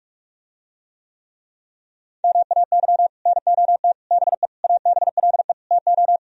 Morse code, heard as a keyed tone, about 44 wpm, 700 Hz. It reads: MAY NOT BE ABLE TO